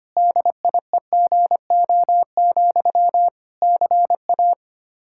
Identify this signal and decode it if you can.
Morse code, heard as a keyed tone, about 25 wpm, 700 Hz. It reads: DIEGO, CA
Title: Practice QSO - Exchange 1 - Beginning - 25wpm